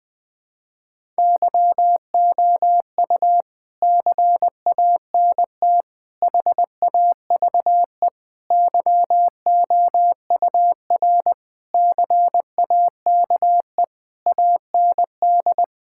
Morse code, heard as a keyed tone, about 20 wpm, 700 Hz. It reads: YOU CANT HAVE YOUR CAKE AND
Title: Sentences with 9-Word Phrases - 20wpm